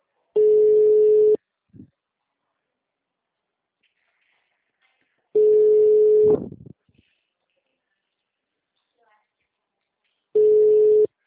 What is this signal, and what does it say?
Ringback tone (Europe, 425 Hz), with speech.